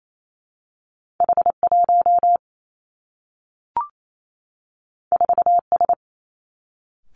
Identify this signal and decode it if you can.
Morse code, heard as a keyed tone, about 28 wpm, 700 Hz. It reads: H 1 4 S
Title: Two Challenging Letter-Number Combinations 28wpm